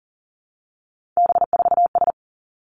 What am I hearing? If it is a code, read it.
Morse code, heard as a keyed tone, about 42 wpm, 700 Hz. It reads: B4S
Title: Three Challenging Letter-Number Combinations 40wpm